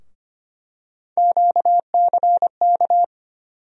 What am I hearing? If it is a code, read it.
Morse code, heard as a keyed tone, about 25 wpm, 700 Hz. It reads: QCK
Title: USA Call Sign Postfixes - 3 Letters 25wpm